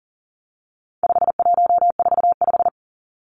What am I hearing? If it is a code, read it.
Morse code, heard as a keyed tone, about 40 wpm, 700 Hz. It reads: 5145
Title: Numbers - 4-Digit 40wpm